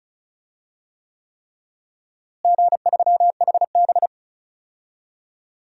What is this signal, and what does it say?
Morse code, heard as a keyed tone, about 34 wpm, 700 Hz. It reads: G3HB
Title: DXCC-100 Call Signs - 4 Characters - Repeated 3x - 35wpm